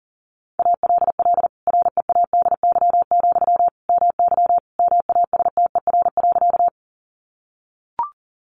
Morse code, heard as a keyed tone, about 40 wpm, 700 Hz. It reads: ALL READY, MY MASTER.
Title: Sentences from Top 1300 Words 40wpm